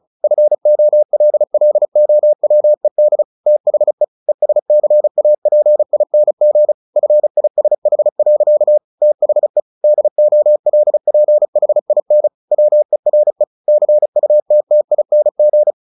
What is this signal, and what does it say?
Morse code, heard as a keyed tone, about 35 wpm, 600 Hz. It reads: FOLLOWED THE ESCAPING FISH. THE DOLPHIN WERE CUTTING
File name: the old man and the sea / the old man and the sea 1 35wpm